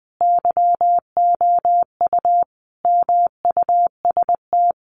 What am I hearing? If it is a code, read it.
Morse code, heard as a keyed tone, about 20 wpm, 700 Hz. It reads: YOU MUST
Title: Sentences from Top 1500 Words 20wpm